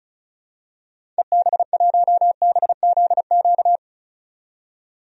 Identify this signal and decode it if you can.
Morse code, heard as a keyed tone, about 36 wpm, 700 Hz. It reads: EB1BZQ
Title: DXCC-100 Call Signs - 6 Characters - 35wpm